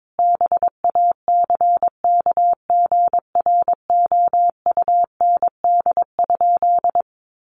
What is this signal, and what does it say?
Morse code, heard as a keyed tone, about 22 wpm, 700 Hz. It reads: BACKGROUND?
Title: Sentences from Top 1300 Words 22wpm